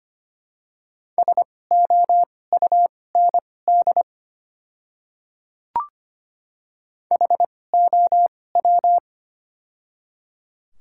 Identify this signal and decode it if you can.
Morse code, heard as a keyed tone, about 25 wpm, spaced as for 18 wpm, 700 Hz. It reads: SOUND HOW
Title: 2x Character Spacing - Top 200 Words - 25wpm